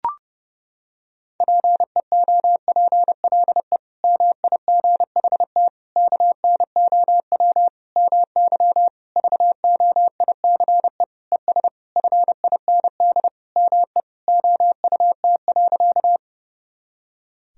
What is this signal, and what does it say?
Morse code, heard as a keyed tone, about 32 wpm, 700 Hz. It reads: PEOPLE MIGHT KNOW MY VOICE ES FIND ME OUT.
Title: Sentences from Top 500 Words and Abbreviations 30wpm